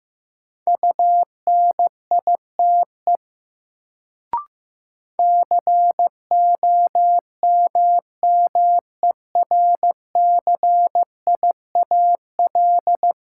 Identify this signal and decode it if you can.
Morse code, heard as a keyed tone, about 15 wpm, 700 Hz. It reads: UNITE COMMERCIAL
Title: Sentences from Top 1000 Words - Review Part 5 - New Word Drill - 15wpm